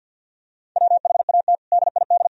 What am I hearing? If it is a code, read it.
Morse code, heard as a keyed tone, about 50 wpm, 700 Hz. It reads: WHAT DID
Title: Sentences from Top 1000 Words - Review Part 2 - 50wpm